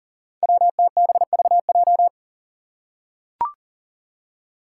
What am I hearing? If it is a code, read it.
Morse code, heard as a keyed tone, about 40 wpm, 700 Hz. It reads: WTBVJ
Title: Five Challenging Letter-Number Combinations 40wpm